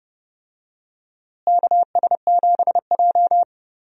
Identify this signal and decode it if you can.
Morse code, heard as a keyed tone, about 29 wpm, 700 Hz. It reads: KS7J
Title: USA Call Signs - 4 Characters - Repeated 2x - 30wpm